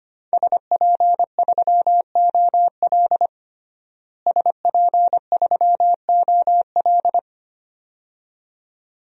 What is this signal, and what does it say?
Morse code, heard as a keyed tone, about 25 wpm, 700 Hz. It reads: SP3OL SP3OL
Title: DXCC-100 Call Signs - 5 Characters - Repeated 2x - 25wpm